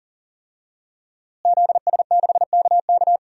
Morse code, heard as a keyed tone, about 40 wpm, 700 Hz. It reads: ZS6KK
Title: DXCC-100 Call Signs - 5 Characters - Repeated 3x - 40wpm